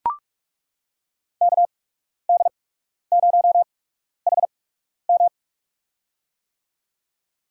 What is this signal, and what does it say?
Morse code, heard as a keyed tone, about 41 wpm, spaced as for 12 wpm, 700 Hz. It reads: KD0HM
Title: USA Call Signs - 5 Characters - 8x Character Spacing - 45wpm